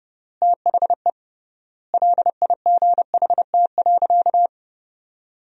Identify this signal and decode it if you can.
Morse code, heard as a keyed tone, about 32 wpm, 700 Hz. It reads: THE LIGHT.
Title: 3x Word-Spacing - Sentences from Top 400 Words - 30wpm